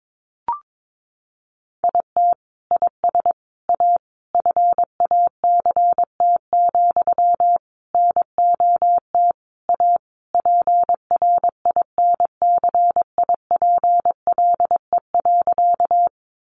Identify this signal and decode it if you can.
Morse code, heard as a keyed tone, about 22 wpm, 700 Hz. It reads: IT IS A FACT, NOT A PRINCIPLE.
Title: Sentences from Top 1400 Words 22wpm